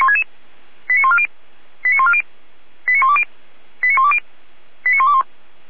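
Five-tone selective call (ZVEI), heard as quick runs of five tones, tones of about 70 ms. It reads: 89140 89140 89120 89120 89121
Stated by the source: ZVEI five-tone selective call 89119 received by SDR.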